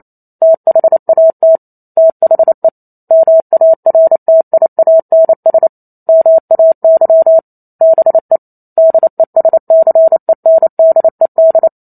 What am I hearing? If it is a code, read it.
Morse code, heard as a keyed tone, about 28 wpm, 650 Hz. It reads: THAT THE MARTIANS MAY BE DESCENDED